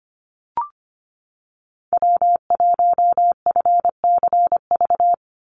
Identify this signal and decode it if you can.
Morse code, heard as a keyed tone, about 25 wpm, 700 Hz. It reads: W1FCV
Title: Call Signs - 25wpm